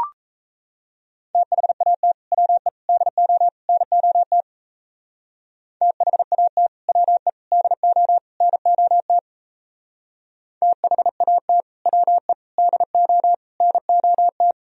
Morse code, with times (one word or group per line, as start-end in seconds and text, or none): 1.3-2.1 THAT
2.3-2.7 WE
2.9-3.5 DO
3.7-4.4 NOT
5.8-6.7 THAT
6.9-7.3 WE
7.5-8.2 DO
8.4-9.2 NOT
10.6-11.6 THAT
11.9-12.3 WE
12.6-13.3 DO
13.6-14.5 NOT